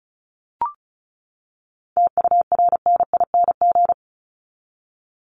Morse code, heard as a keyed tone, about 34 wpm, 700 Hz. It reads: TURNING
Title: ING Suffix - Top 100 Words with Base Words 35wpm